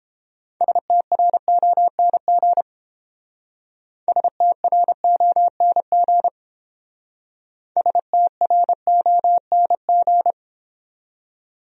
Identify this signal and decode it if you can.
Morse code, heard as a keyed tone, about 29 wpm, 700 Hz. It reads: STRONG STRONG STRONG